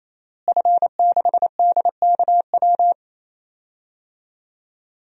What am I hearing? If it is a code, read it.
Morse code, heard as a keyed tone, about 28 wpm, 700 Hz. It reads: F6DKW
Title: DXCC-100 Call Signs - 5 Characters - Repeated 3x - 28wpm